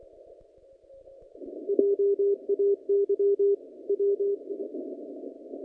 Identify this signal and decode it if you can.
Morse code, heard as a keyed tone, about 26 wpm, 400 Hz. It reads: JAY W
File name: contests / naqp-cw-k5tr-op-k5pi- -0300